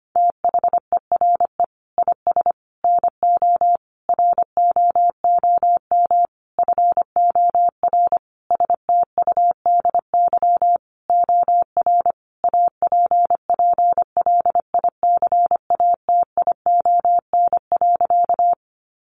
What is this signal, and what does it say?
Morse code, heard as a keyed tone, about 25 wpm, 700 Hz. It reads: THERE IS NO ROOM FOR STUDY OR APPLICATION.
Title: Sentences from Top 1200 Words 25wpm